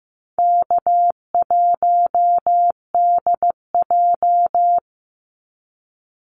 Morse code, heard as a keyed tone, about 15 wpm, 700 Hz. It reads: K1DJ
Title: Easy Call Signs Repeated 2x - 15wpm